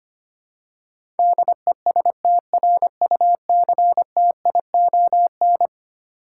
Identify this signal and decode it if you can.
Morse code, heard as a keyed tone, about 25 wpm, 700 Hz. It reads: DESTRUCTION
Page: morse-code-ninja-practice-25wpm